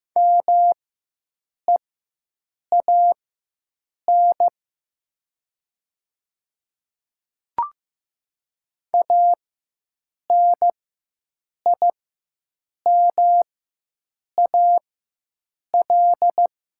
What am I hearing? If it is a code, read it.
Morse code, heard as a keyed tone, about 15 wpm, spaced as for 7 wpm, 700 Hz. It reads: MEAN ANIMAL
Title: 4x Character Spacing - Top 500 Words - 15wpm